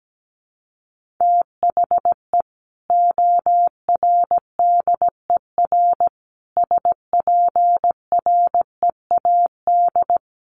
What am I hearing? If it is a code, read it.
Morse code, heard as a keyed tone, about 17 wpm, 700 Hz. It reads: THE ORDER SPREAD
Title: Sentences from Top 1200 Words 17wpm